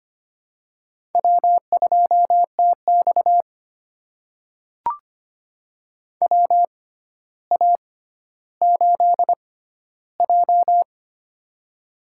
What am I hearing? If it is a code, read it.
Morse code, heard as a keyed tone, about 25 wpm, spaced as for 13 wpm, 700 Hz. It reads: W2TX WA8J
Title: USA Call Signs - 4 Characters - 6x Character Spacing - 25wpm